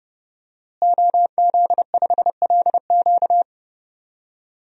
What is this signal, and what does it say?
Morse code, heard as a keyed tone, about 32 wpm, 700 Hz. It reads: OZ5LQ
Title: DXCC-100 Call Signs - 5 Characters - Repeated 3x - 30wpm